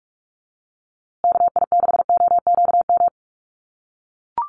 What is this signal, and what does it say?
Morse code, heard as a keyed tone, about 44 wpm, 700 Hz. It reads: KI6OQM
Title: USA Call Signs Repeated 3x 45wpm